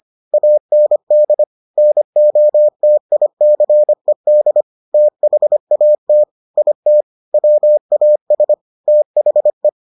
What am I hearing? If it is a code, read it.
Morse code, heard as a keyed tone, about 25 wpm, 600 Hz. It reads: AND NOTICED THAT IT WAS THE